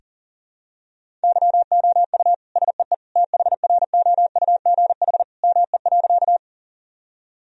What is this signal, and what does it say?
Morse code, heard as a keyed tone, about 40 wpm, 700 Hz. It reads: YOU SEE THROUGH ME.
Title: Sentences from Top 1000 Words - Review Part 1 - 40wpm